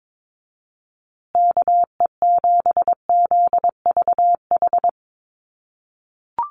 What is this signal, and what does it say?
Morse code, heard as a keyed tone, about 22 wpm, 700 Hz. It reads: KE7ZVH
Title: USA Call Signs Repeated 3x 22wpm